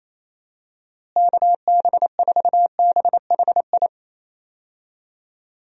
Morse code, heard as a keyed tone, about 28 wpm, 700 Hz. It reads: KB4BHI
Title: USA Call Signs - 6 Characters - Repeated 2x - 28wpm